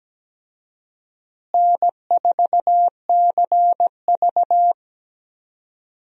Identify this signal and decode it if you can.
Morse code, heard as a keyed tone, about 17 wpm, 700 Hz. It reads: N4CV
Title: USA Call Signs - 4 Characters - Repeated 3x - 17wpm